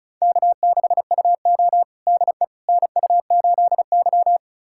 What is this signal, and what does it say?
Morse code, heard as a keyed tone, about 34 wpm, 700 Hz. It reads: K6UO DE NU8Y
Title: USA Callsign DE Callsign - 4 Characters - 35wpm